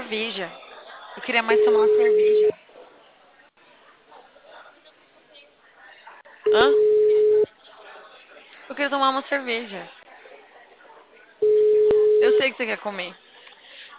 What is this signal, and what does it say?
Ringback tone (Europe, 425 Hz), with speech.
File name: out- -2030- - - .16